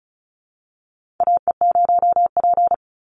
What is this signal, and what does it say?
Morse code, heard as a keyed tone, about 34 wpm, 700 Hz. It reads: AE0P
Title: Call Signs Repeated 3x - 35wpm